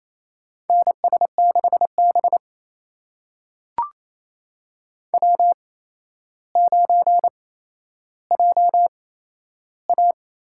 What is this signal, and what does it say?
Morse code, heard as a keyed tone, about 28 wpm, spaced as for 12 wpm, 700 Hz. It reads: NS6B W9JA
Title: USA Call Signs - 4 Characters - 8x Character Spacing - 28wpm